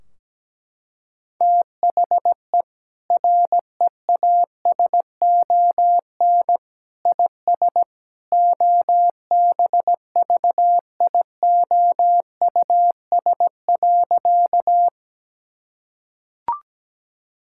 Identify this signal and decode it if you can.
Morse code, heard as a keyed tone, about 17 wpm, 700 Hz. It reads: THE REASON IS OBVIOUS.
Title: Sentences from Top 1600 Words 17wpm